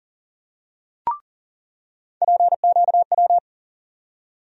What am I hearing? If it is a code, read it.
Morse code, heard as a keyed tone, about 40 wpm, 700 Hz. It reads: PQW
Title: USA Call Sign Postfixes - 3 Letters 40wpm